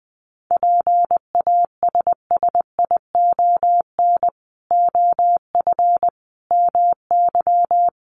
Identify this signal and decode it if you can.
Morse code, heard as a keyed tone, about 20 wpm, 700 Hz. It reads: PASSION OF MY
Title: Sentences from Top 1300 Words 20wpm